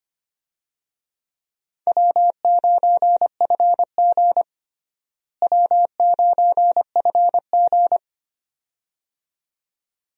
Morse code, heard as a keyed tone, about 25 wpm, 700 Hz. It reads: W9FG W9FG
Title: USA Call Signs - 4 Characters - Repeated 2x - 25wpm